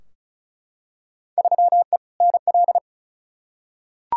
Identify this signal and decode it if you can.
Morse code, heard as a keyed tone, about 36 wpm, 700 Hz. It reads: 3E NL
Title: Field Day Exchanges Encore - 35wpm